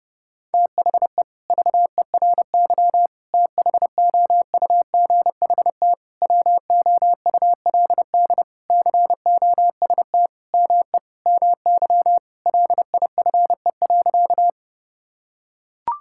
Morse code, heard as a keyed tone, about 32 wpm, 700 Hz. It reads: THE VERY THOUGHT WOULD COST ME MY LIFE.
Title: Sentences from Top 1000 Words - Review Part 5 - 30wpm